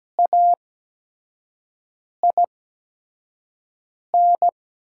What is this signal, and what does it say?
Morse code, heard as a keyed tone, about 17 wpm, spaced as for 5 wpm, 700 Hz. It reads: AIN
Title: 8x Character Spacing - Top 300 Words - 17wpm